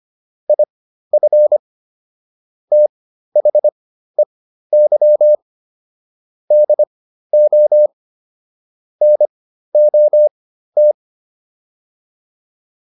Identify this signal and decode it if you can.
Morse code, heard as a keyed tone, about 25 wpm, spaced as for 13 wpm, 600 Hz. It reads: IF THEY DO NOT